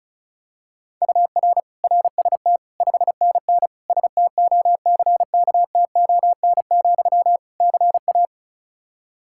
Morse code, heard as a keyed tone, about 36 wpm, 700 Hz. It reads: UR RST 5NN STOCKTON, CA
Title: K1USN SST QSO Practice 35wpm